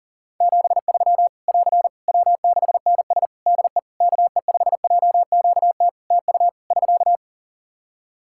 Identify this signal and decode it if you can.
Morse code, heard as a keyed tone, about 40 wpm, 700 Hz. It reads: 73 <AR> W6NS DE KE5JQT TU <SK>
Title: Practice QSO - Exchange 4 - Ending - 40wpm